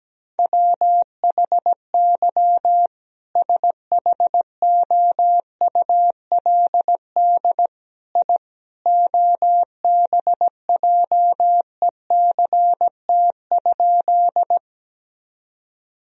Morse code, heard as a keyed tone, about 17 wpm, 700 Hz. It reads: WHY SHOULD I OBJECT?
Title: Sentences from Top 1000 Words - Review Part 3 - 17wpm